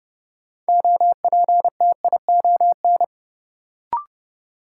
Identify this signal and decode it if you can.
Morse code, heard as a keyed tone, about 31 wpm, 700 Hz. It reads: OPTION
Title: Sentences from Top 1300 Words - New Word Drill Alt - 30wpm